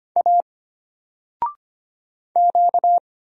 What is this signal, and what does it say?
Morse code, heard as a keyed tone, about 25 wpm, 700 Hz. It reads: A Q
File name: Single-Letter-Number-25wpm